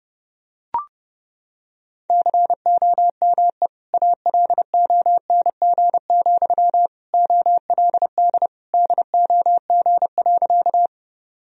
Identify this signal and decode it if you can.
Morse code, heard as a keyed tone, about 32 wpm, 700 Hz. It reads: COME ALONG, OLD DOG.
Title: Sentences from Top 1000 Words - Review Part 2 - 30wpm